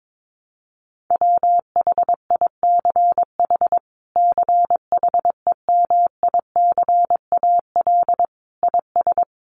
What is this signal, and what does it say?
Morse code, heard as a keyed tone, about 22 wpm, 700 Hz. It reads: WHICH CHEMICAL IS
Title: Sentences from Top 1200 Words 22wpm